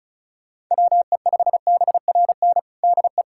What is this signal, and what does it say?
Morse code, heard as a keyed tone, about 36 wpm, 700 Hz. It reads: WE5BRN DE
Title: USA Callsign DE Callsign - 6 Characters - 35wpm